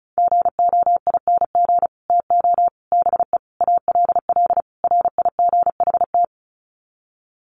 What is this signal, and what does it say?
Morse code, heard as a keyed tone, about 34 wpm, 700 Hz. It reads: GOING TO BE ALL RIGHT
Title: Sets of 5 Words Encore2 35wpm